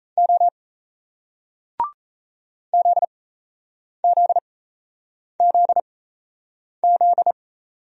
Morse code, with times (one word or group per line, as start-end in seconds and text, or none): 0.2-0.5 O
2.7-7.3 ZZZZ